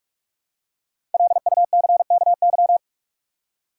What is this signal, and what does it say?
Morse code, heard as a keyed tone, about 41 wpm, 700 Hz. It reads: LUCKY
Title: Five Letter Words - 45wpm